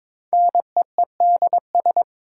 Morse code, heard as a keyed tone, about 22 wpm, 700 Hz. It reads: NEEDS
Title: Sentences from Top 1000 Words - Review Part 3 - New Word Drill - 22wpm